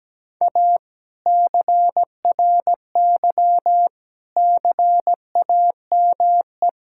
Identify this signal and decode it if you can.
Morse code, heard as a keyed tone, about 17 wpm, 700 Hz. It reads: A CRY CAME
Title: Sentences from Top 1000 Words - Review Part 5 - 17wpm